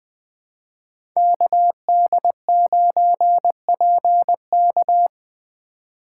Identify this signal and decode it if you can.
Morse code, heard as a keyed tone, about 20 wpm, 700 Hz. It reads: KD9PK
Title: USA Call Signs - 5 Characters - 3x Character Spacing - 20wpm